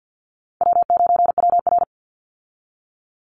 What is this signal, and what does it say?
Morse code, heard as a keyed tone, about 48 wpm, 700 Hz. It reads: W9WR
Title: Easy Call Signs Repeated 2x - 50wpm